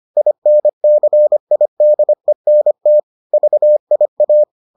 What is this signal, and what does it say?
Morse code, heard as a keyed tone, about 25 wpm, 600 Hz. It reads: INCIDENT VIA